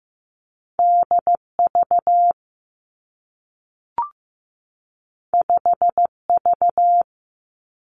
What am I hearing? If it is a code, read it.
Morse code, heard as a keyed tone, about 15 wpm, 700 Hz. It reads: D V 5 V
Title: Two Challenging Letter-Number Combinations 15wpm